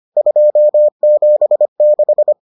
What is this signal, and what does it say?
Morse code, heard as a keyed tone, about 25 wpm, 600 Hz. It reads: 276